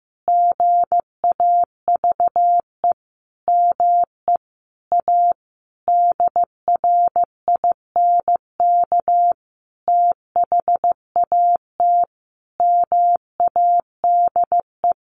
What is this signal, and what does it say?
Morse code, heard as a keyed tone, about 15 wpm, 700 Hz. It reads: GAVE ME A DRINK THAT MADE
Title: Sentences from Top 1500 Words 15wpm